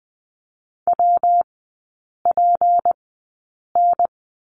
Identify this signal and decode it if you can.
Morse code, heard as a keyed tone, about 20 wpm, spaced as for 8 wpm, 700 Hz. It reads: WPN
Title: 152 Lesson 25e - Letter p - Three Characters 20wpm